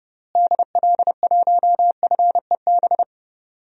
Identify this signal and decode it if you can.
Morse code, heard as a keyed tone, about 30 wpm, 700 Hz. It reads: DL1FEB
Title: DXCC-100 Call Signs - 6 Characters - Repeated 3x - 30wpm